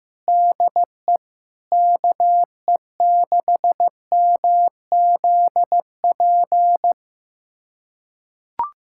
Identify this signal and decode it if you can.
Morse code, heard as a keyed tone, about 15 wpm, 700 Hz. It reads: DE KE6MZP
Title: USA Callsign DE Callsign - 6 Characters - 15wpm